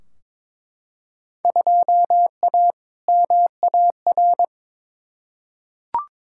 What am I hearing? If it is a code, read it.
Morse code, heard as a keyed tone, about 22 wpm, 700 Hz. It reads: 2A MAR
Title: Field Day Exchanges - 22wpm